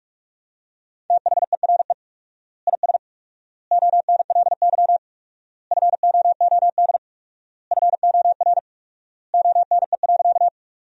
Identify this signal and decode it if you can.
Morse code, heard as a keyed tone, about 46 wpm, 700 Hz. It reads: THERE IS ONLY FOOD FOR ONE.
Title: 4x Word-Spacing - Sentences from Top 300 Words - 45wpm